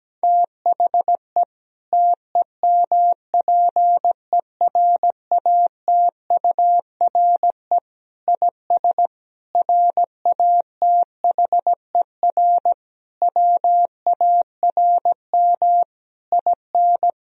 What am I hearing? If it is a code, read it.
Morse code, heard as a keyed tone, about 17 wpm, 700 Hz. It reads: THE TEMPERATURE IS RATHER WARM IN
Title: Sentences from Top 1000 Words - Review Part 4 - 17wpm